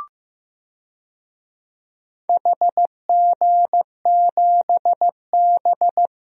Morse code, heard as a keyed tone, about 15 wpm, 700 Hz. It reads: HG7B
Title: DXCC-100 Call Signs - 4 Characters - Repeated 2x - 15wpm